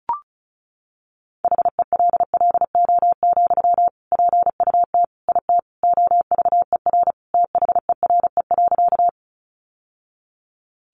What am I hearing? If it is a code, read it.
Morse code, heard as a keyed tone, about 34 wpm, 700 Hz. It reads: HELLO, PUT IT OVER THERE.
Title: Sentences from Top 1100 Words 35wpm